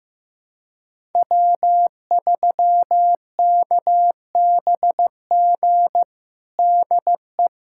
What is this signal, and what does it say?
Morse code, heard as a keyed tone, about 15 wpm, 700 Hz. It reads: W3KBG DE
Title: USA Callsign DE Callsign - 5 Characters - 15wpm